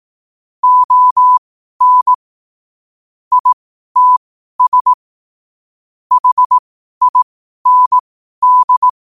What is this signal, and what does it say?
Morse code, heard as a keyed tone, about 18 wpm, spaced as for 12 wpm, 1000 Hz. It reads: ON ITS HIND
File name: qotdfiles / 12wpm